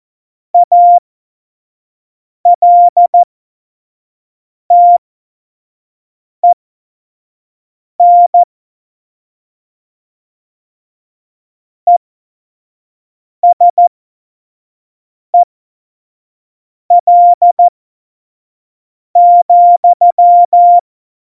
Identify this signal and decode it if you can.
Morse code, heard as a keyed tone, about 14 wpm, spaced as for 5 wpm, 700 Hz. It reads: ALTEN ESEL,
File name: Morsekurs / Witze / Witz 13. Oktober 2020, 25 F70